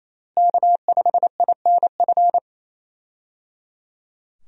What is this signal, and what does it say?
Morse code, heard as a keyed tone, about 28 wpm, 700 Hz. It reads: K5INF
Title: USA Call Signs - 5 Characters - Repeated 3x - 28wpm